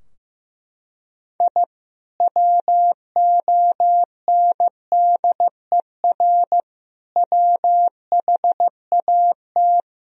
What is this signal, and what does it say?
Morse code, heard as a keyed tone, about 16 wpm, 700 Hz. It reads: I WONDER WHAT
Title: Sentences from Top 1000 Words - Review Part 3 - 15wpm